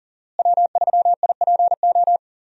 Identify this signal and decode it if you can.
Morse code, heard as a keyed tone, about 37 wpm, 700 Hz. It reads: W3IPO
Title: USA Call Signs - 5 Characters - Repeated 3x - 40wpm